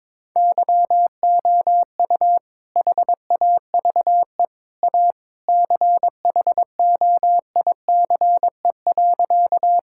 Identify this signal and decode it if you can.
Morse code, heard as a keyed tone, about 22 wpm, 700 Hz. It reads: YOU HAVE A CHOICE.
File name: practice / 22wpm / Sentences with 3-Word Phrases - 22wpm